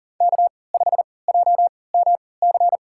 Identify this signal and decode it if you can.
Morse code, heard as a keyed tone, about 40 wpm, spaced as for 23 wpm, 700 Hz. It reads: K5JMC
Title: USA Call Signs - 5 Characters - 3x Character Spacing - 40wpm